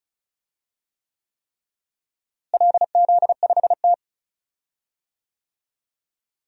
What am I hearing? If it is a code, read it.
Morse code, heard as a keyed tone, about 34 wpm, 700 Hz. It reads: LZ5T
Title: DXCC-100 Call Signs - 4 Characters - Repeated 3x - 35wpm